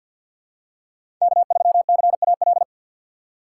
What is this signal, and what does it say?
Morse code, heard as a keyed tone, about 50 wpm, 700 Hz. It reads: K3CAL
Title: USA Call Signs - 5 Characters - 4x Character Spacing - 50wpm